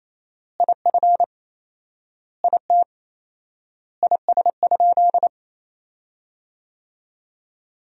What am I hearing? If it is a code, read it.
Morse code, heard as a keyed tone, about 27 wpm, 700 Hz. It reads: IF IT IS?
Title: 4x Word-Spacing - Sentences from Top 100 Words - 28wpm